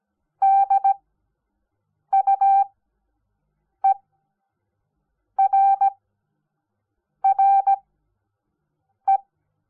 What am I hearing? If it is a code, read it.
Morse code, heard as a keyed tone, about 17 wpm, spaced as for 6 wpm, 780 Hz. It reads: DUERRE
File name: Onlinelehrgang / morsen / DJ4UF Morsen05